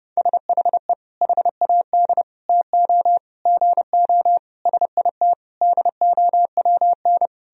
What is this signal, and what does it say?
Morse code, heard as a keyed tone, about 32 wpm, 700 Hz. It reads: SHE HAD TO GO SIT DOWN
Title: Sentences from Top 1000 Words - Review Part 2 - 30wpm